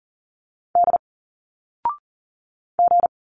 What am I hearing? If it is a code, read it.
Morse code, heard as a keyed tone, about 39 wpm, 700 Hz. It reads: D G
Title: Single Letters 40wpm